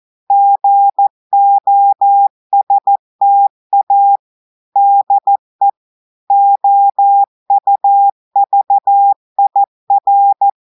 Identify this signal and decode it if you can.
Morse code, heard as a keyed tone, about 14 wpm, 800 Hz. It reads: GOSTA DE OUVIR
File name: audios / portugais / portugais texte en clair 2 vitesse 14 mots minutes